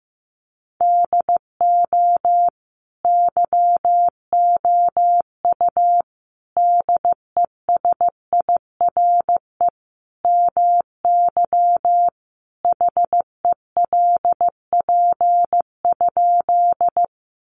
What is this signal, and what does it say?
Morse code, heard as a keyed tone, about 15 wpm, 700 Hz. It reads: DO YOU DESIRE MY HELP?
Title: Sentences from Top 1100 Words 15wpm